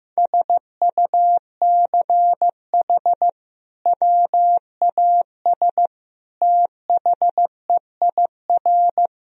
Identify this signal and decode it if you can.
Morse code, heard as a keyed tone, about 15 wpm, 700 Hz. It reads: SUCH WAS THEIR
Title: Sentences from Top 1000 Words - Review Part 2 - 15wpm